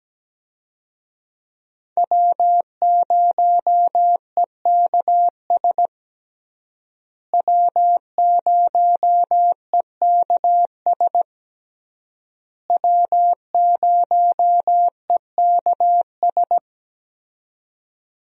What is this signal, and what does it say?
Morse code, heard as a keyed tone, about 17 wpm, 700 Hz. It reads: W0EKS W0EKS W0EKS
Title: USA Call Signs - 5 Characters - Repeated 3x - 17wpm